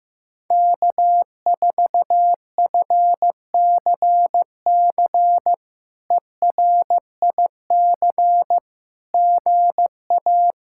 Morse code, heard as a keyed tone, about 15 wpm, 700 Hz. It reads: K4FCC ERIC GA
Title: K1USN SST QSO Practice - 15wpm